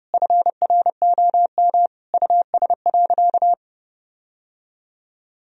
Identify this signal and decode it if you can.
Morse code, heard as a keyed tone, about 30 wpm, 700 Hz. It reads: FROM US.
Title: Sentences from Top 1000 Words - Review Part 1 - 30wpm